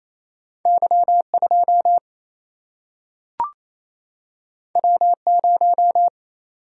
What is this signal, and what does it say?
Morse code, heard as a keyed tone, about 28 wpm, 700 Hz. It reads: Y 2 W 0
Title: Single-Letter Single-Number 28wpm